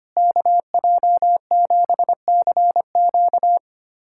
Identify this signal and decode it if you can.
Morse code, heard as a keyed tone, about 25 wpm, 700 Hz. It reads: KJ7CQ
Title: USA Callsign DE Callsign - 5 Characters - 25wpm